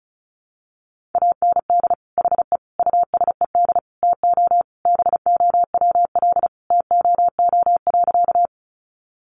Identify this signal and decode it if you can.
Morse code, heard as a keyed tone, about 34 wpm, 700 Hz. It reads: AND HE USED TO BOWL TOO.
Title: Sentences from Top 1200 Words 35wpm